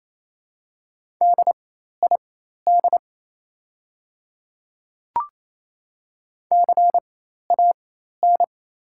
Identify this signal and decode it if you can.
Morse code, heard as a keyed tone, about 28 wpm, spaced as for 13 wpm, 690 Hz. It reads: DID CAN
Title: 4x Character Spacing - Top 300 Words - 28wpm